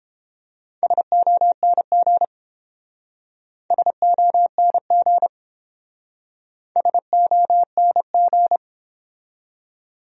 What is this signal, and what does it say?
Morse code, heard as a keyed tone, about 29 wpm, 700 Hz. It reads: SONG SONG SONG